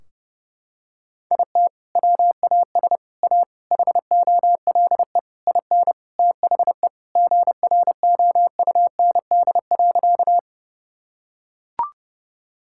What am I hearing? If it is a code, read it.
Morse code, heard as a keyed tone, about 32 wpm, 700 Hz. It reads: IT WAS A HOLE IN THE GROUND.
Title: Sentences from Top 1600 Words 30wpm